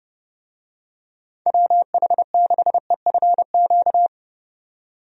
Morse code, heard as a keyed tone, about 32 wpm, 700 Hz. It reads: WH6EFQ
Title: USA Call Signs - 6 Characters - 30wpm